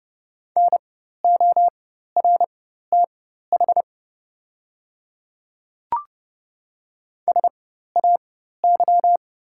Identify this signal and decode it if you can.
Morse code, heard as a keyed tone, about 30 wpm, spaced as for 14 wpm, 690 Hz. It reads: NORTH SAY
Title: 4x Character Spacing - Top 300 Words - 30wpm